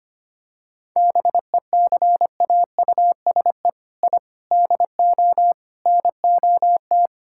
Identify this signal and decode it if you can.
Morse code, heard as a keyed tone, about 25 wpm, 700 Hz. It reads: BECAUSE I DO NOT
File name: practice / 25wpm / Sentences with 4-Word Phrases - 25wpm